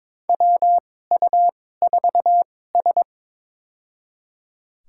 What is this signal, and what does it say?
Morse code, heard as a keyed tone, about 22 wpm, spaced as for 16 wpm, 700 Hz. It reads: WU4S
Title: USA Call Signs - 4 Characters - 2x Character Spacing - 22wpm